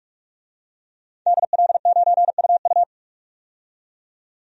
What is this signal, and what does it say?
Morse code, heard as a keyed tone, about 42 wpm, 700 Hz. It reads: DL9UU
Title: DXCC-100 Call Signs - 5 Characters - 45wpm